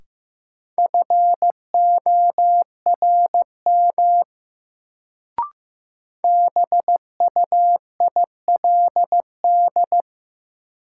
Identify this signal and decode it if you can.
Morse code, heard as a keyed tone, about 15 wpm, 700 Hz. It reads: FORM BUILD
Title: Sentences from Top 1000 Words - Review Part 1 - New Word Drill - 15wpm